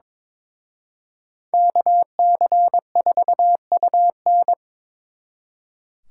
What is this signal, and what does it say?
Morse code, heard as a keyed tone, about 26 wpm, 700 Hz. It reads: KC4UN
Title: Speed Racing - USA 2x2 Call Signs 22wpm